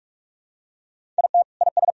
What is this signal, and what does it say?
Morse code, heard as a keyed tone, about 47 wpm, 700 Hz. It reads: IT IS
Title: Sentences from Top 1000 Words - Review Part 1 - 45wpm